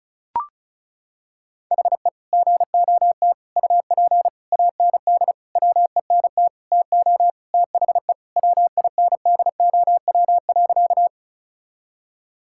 Morse code, heard as a keyed tone, about 36 wpm, 700 Hz. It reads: HE GOT UP AND WENT TO THE WINDOW.
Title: Sentences from Top 1600 Words 35wpm